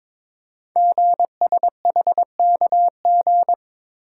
Morse code, heard as a keyed tone, about 22 wpm, 690 Hz. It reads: GSHKG
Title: Code Groups - 5 Letters - 22wpm